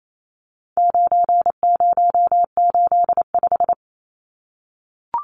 Morse code, heard as a keyed tone, about 28 wpm, 700 Hz. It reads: 9085
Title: Numbers - 4-Digit 28wpm